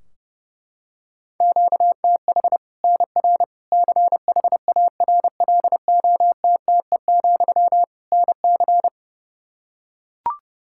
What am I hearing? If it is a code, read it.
Morse code, heard as a keyed tone, about 29 wpm, 700 Hz. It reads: QTH NR CHARLOTTE, NC
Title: Practice QSO - Exchange 1 - Beginning - 30wpm